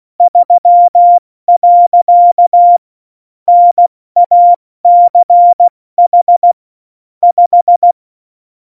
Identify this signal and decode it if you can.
Morse code, heard as a keyed tone, about 16 wpm, 700 Hz. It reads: 3. NACH 5
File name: Morsekurs / Witze / Witz 18 Juli 17, Tempo 70 F80